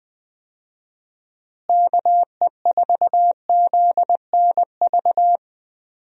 Morse code, heard as a keyed tone, about 20 wpm, 700 Hz. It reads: KE4ZNV
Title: USA Call Signs - 6 Characters - Repeated 3x - 20wpm